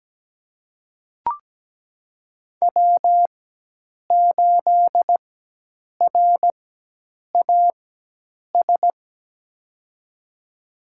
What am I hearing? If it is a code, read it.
Morse code, heard as a keyed tone, about 17 wpm, spaced as for 8 wpm, 700 Hz. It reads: W8RAS
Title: USA Call Signs - 4x Character Spacing - 17wpm